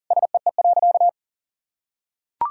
Morse code, heard as a keyed tone, about 40 wpm, 700 Hz. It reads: SEE.
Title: Sentences from Top 200 Words and Abbreviations 40wpm